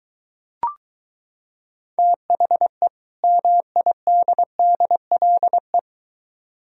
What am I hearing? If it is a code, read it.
Morse code, heard as a keyed tone, about 22 wpm, 700 Hz. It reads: THE MIDDLE